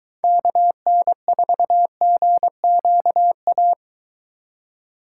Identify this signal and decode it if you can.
Morse code, heard as a keyed tone, about 23 wpm, 700 Hz. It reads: KN4GQA